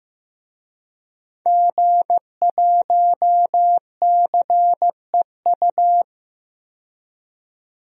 Morse code, heard as a keyed tone, about 15 wpm, 700 Hz. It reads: G1CEU
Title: DXCC-100 Call Signs - 5 Characters - 15wpm